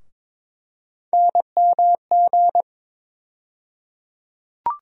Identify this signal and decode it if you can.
Morse code, heard as a keyed tone, about 22 wpm, 700 Hz. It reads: NMG
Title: USA Call Sign Postfixes - 3 Letters 22wpm